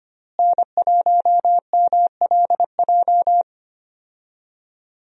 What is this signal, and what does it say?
Morse code, heard as a keyed tone, about 25 wpm, 700 Hz. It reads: N1MLJ
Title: USA Call Signs - 5 Characters - Repeated 3x - 25wpm